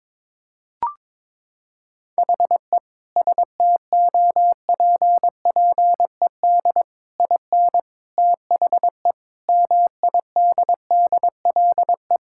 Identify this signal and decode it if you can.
Morse code, heard as a keyed tone, about 22 wpm, 700 Hz. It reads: HE STOPPED IN THE MIDDLE
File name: practice / 22wpm / Sentences with 3-Word Phrases - 22wpm